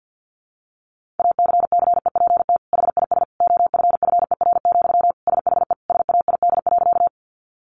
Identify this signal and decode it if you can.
Morse code, heard as a keyed tone, about 50 wpm, 700 Hz. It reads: ACCEPT HIS OFFER, SHE SAID.
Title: Sentences from Top 1300 Words 50wpm